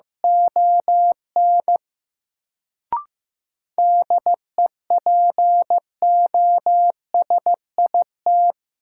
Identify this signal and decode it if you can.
Morse code, heard as a keyed tone, about 15 wpm, 700 Hz. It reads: ON DEPOSIT
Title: Sentences from Top 1300 Words - New Word Drill - 15wpm